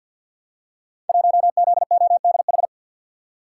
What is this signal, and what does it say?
Morse code, heard as a keyed tone, about 52 wpm, 700 Hz. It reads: 1ZODH
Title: Code Groups - 5 Letter:Numbers - 50wpm